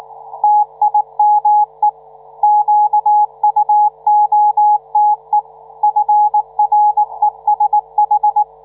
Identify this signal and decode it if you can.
Morse code, heard as a keyed tone, about 19 wpm, 840 Hz, with radio noise. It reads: TIME QUOTE FRESH